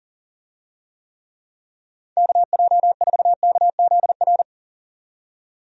Morse code, heard as a keyed tone, about 40 wpm, 700 Hz. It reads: KJ4KZR
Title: USA Call Signs - 6 Characters - 40wpm